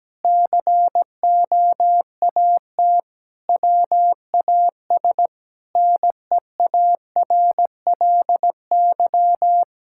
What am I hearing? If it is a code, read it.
Morse code, heard as a keyed tone, about 17 wpm, 700 Hz. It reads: COAT WAS NEARLY
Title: Sentences from Top 1600 Words 17wpm